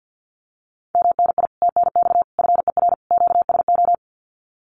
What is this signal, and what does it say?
Morse code, heard as a keyed tone, about 50 wpm, 700 Hz. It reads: MNI TNX FER QSO
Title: Common QSO Phrases Encore 50wpm